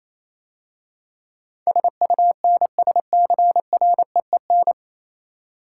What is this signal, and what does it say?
Morse code, heard as a keyed tone, about 27 wpm, 700 Hz. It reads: SUNSCREEN
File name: Easy Words - Speed-Racing with Hints - 22wpm